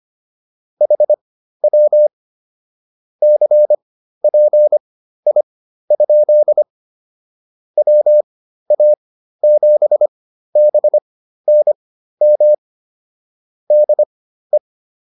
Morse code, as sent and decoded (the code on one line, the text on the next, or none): .... .-- / -.-. .--. .. ..--.. / .-- .- --... -... -. -- / -.. .
HW CPI? WA7BNM DE